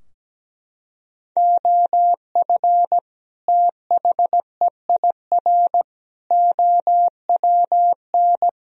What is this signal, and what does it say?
Morse code, heard as a keyed tone, about 17 wpm, 700 Hz. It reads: OF THEIR OWN
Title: Sentences from Top 1000 Words - Review Part 3 - 17wpm